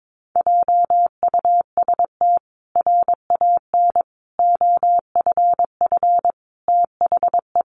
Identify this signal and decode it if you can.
Morse code, heard as a keyed tone, about 22 wpm, 700 Hz. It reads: JUST RAN OFF THE
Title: A Sound Of Thunder 22wpm